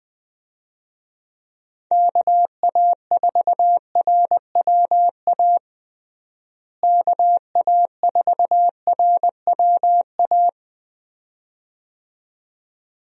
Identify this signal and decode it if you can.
Morse code, heard as a keyed tone, about 20 wpm, 700 Hz. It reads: KA4RWA KA4RWA
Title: USA Call Signs - 6 Characters - Repeated 2x - 20wpm